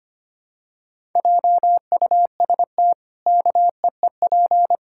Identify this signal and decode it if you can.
Morse code, heard as a keyed tone, about 25 wpm, 700 Hz. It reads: JUST KEEP
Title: Pangram Challenge v2 - 25wpm